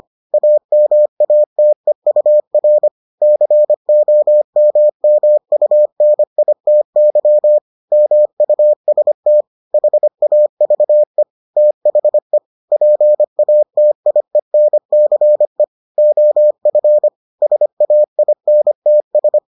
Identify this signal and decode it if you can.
Morse code, heard as a keyed tone, about 25 wpm, 600 Hz. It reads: AMATEUR COMMUNITY MUST HAVE THE PATIENCE OF SAINTS